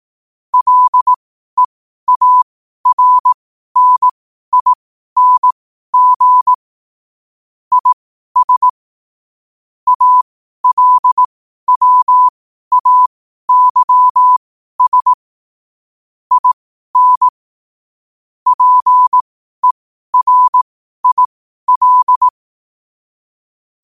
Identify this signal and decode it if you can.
Morse code, heard as a keyed tone, about 18 wpm, spaced as for 12 wpm, 1000 Hz. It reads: LEARNING IS ALWAYS IN PERIL
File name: qotdfiles / 12wpm